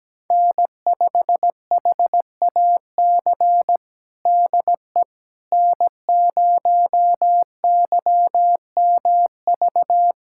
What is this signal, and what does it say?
Morse code, heard as a keyed tone, about 17 wpm, 700 Hz. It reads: N5HAC DE N0YMV
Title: USA Callsign DE Callsign - 5 Characters - 17wpm